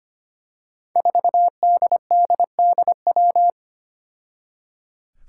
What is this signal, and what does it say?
Morse code, heard as a keyed tone, about 25 wpm, 700 Hz. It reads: 4DDDW
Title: Five Challenging Letter-Number Combinations 25wpm